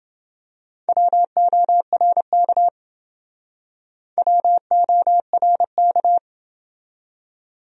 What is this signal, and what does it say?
Morse code, heard as a keyed tone, about 30 wpm, 700 Hz. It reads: WORK WORK